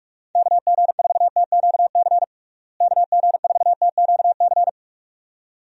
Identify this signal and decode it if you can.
Morse code, heard as a keyed tone, about 46 wpm, 700 Hz. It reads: KG4TQC KG4TQC
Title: USA Call Signs - 6 Characters - Repeated 2x - 45wpm